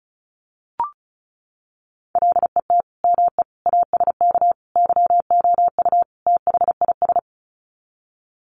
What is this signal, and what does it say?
Morse code, heard as a keyed tone, about 34 wpm, 700 Hz. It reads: LET ME ASK YOU THIS
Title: Sets of 5 Words 35wpm